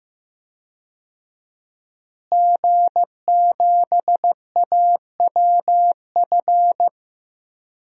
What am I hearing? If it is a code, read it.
Morse code, heard as a keyed tone, about 15 wpm, 700 Hz. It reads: G7AWF
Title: DXCC-100 Call Signs - 5 Characters - Repeated 2x - 15wpm